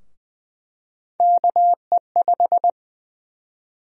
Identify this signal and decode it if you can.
Morse code, heard as a keyed tone, about 21 wpm, 700 Hz. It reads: KE5
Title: USA Call Sign Prefixes - Repeated 2x - 20wpm